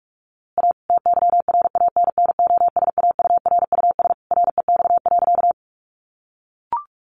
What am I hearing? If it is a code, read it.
Morse code, heard as a keyed tone, about 44 wpm, 700 Hz. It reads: A TYRANNOSAURUS REX.
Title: A Sound Of Thunder 45wpm